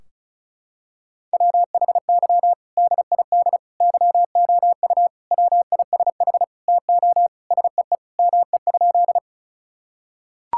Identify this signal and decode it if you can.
Morse code, heard as a keyed tone, about 34 wpm, 700 Hz. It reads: WHY DID YOU WISH TO SEE ME?
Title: Sentences from Top 1600 Words 35wpm